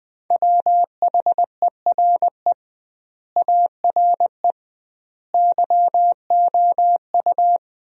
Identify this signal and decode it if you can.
Morse code, heard as a keyed tone, about 20 wpm, 700 Hz. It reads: WHERE ARE YOU
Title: 2x Word-Spacing - Sentences from Top 200 Words - 20wpm